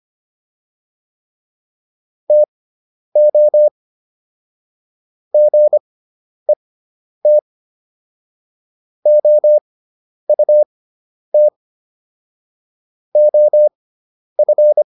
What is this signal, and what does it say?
Morse code, heard as a keyed tone, about 25 wpm, spaced as for 10 wpm, 600 Hz. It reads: TO GET OUT OF